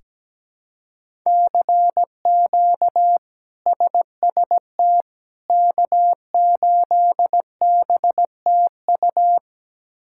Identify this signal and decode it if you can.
Morse code, heard as a keyed tone, about 17 wpm, 700 Hz. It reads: CQ SST K8BTU
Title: K1USN SST QSO Practice - 17wpm